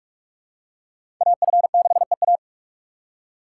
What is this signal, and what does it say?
Morse code, heard as a keyed tone, about 46 wpm, 690 Hz. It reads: AF6EA
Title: USA Call Signs - 5 Characters - 8x Character Spacing - 45wpm